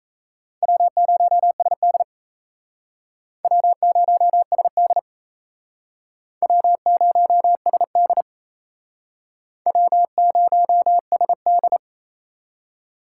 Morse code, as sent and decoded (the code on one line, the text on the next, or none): .-- ----- ... -.. / .-- ----- ... -.. / .-- ----- ... -.. / .-- ----- ... -..
W0SD W0SD W0SD W0SD